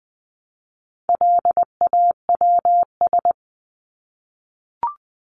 Morse code, heard as a keyed tone, about 20 wpm, 700 Hz. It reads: LAWS
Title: ES:S Suffix - Top 100 Words with Base Words 20wpm